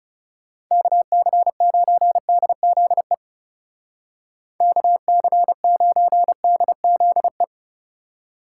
Morse code, heard as a keyed tone, about 34 wpm, 700 Hz. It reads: KC9DZE KC9DZE
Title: Speed Racing - USA Call Signs 30wpm